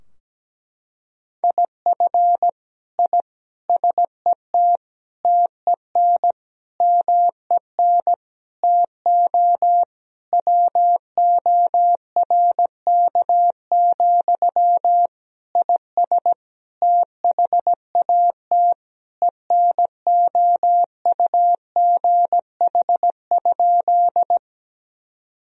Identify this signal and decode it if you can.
Morse code, heard as a keyed tone, about 17 wpm, 700 Hz. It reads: IF I SET TEN MEN TO WORK, IS THAT ENOUGH?
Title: Sentences from Top 1000 Words - Review Part 2 - 17wpm